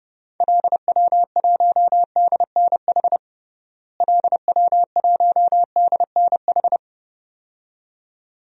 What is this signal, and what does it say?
Morse code, heard as a keyed tone, about 30 wpm, 700 Hz. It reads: LW1DNH LW1DNH
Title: DXCC-100 Call Signs - 6 Characters - Repeated 2x - 30wpm